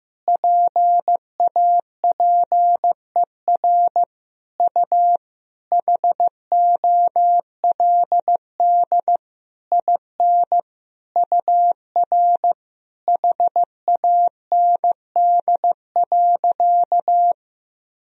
Morse code, heard as a keyed tone, about 15 wpm, 700 Hz. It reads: PAPER U HOLD IN UR HAND.
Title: Sentences from Top 400 Words and Abbreviations 15wpm